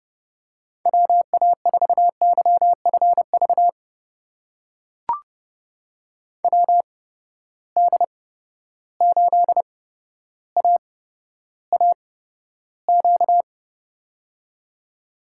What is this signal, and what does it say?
Morse code, heard as a keyed tone, about 30 wpm, spaced as for 13 wpm, 700 Hz. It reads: WA4YFV WD8AAQ
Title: USA Call Signs - 6 Characters - 8x Character Spacing - 30wpm